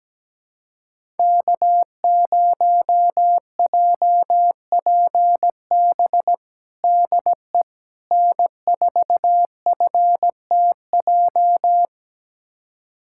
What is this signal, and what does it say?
Morse code, heard as a keyed tone, about 17 wpm, 700 Hz. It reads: K0JPB DE N4FTJ
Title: USA Callsign DE Callsign - 5 Characters - 17wpm